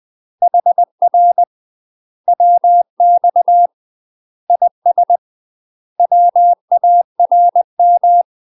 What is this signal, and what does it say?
Morse code, heard as a keyed tone, about 20 wpm, 700 Hz. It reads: HR WX IS WARM
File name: UK MASTER QSOs TRAINING / 20WPM / UK MASTER QSOs 0013